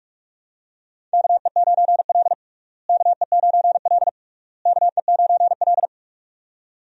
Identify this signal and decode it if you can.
Morse code, heard as a keyed tone, about 46 wpm, 700 Hz. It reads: KE9L KE9L KE9L
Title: USA Call Signs - 4 Characters - Repeated 3x - 45wpm